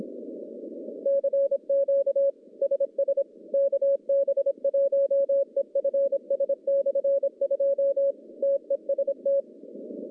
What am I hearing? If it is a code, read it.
Morse code, heard as a keyed tone, about 26 wpm, 550 Hz. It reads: CQ SS KB1EFS/2 TEST
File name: contests / CW-SS-k5tr- -1100